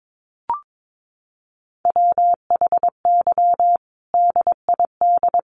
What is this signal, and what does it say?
Morse code, heard as a keyed tone, about 22 wpm, 700 Hz. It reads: WHY DID
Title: Sentences from Top 1400 Words 22wpm